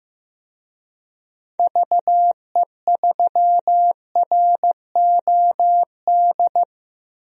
Morse code, heard as a keyed tone, about 15 wpm, 700 Hz. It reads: VE3ROD
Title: DXCC-100 Call Signs - 6 Characters - Repeated 3x - 15wpm